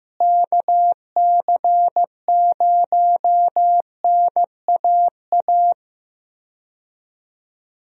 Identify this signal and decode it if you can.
Morse code, heard as a keyed tone, about 15 wpm, 700 Hz. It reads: KC0NAA
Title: USA Call Signs - 6 Characters - Repeated 3x - 15wpm